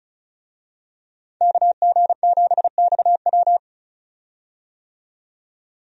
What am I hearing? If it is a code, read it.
Morse code, heard as a keyed tone, about 36 wpm, 700 Hz. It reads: KG7XW
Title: USA Call Signs - 5 Characters - 35wpm